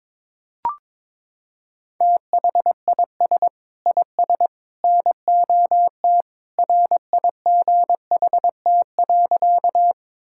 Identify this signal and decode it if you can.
Morse code, heard as a keyed tone, about 23 wpm, 700 Hz. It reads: THIS IS NOT RIGHT.
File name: practice / 22wpm / Sentences with 3-Word Phrases - 22wpm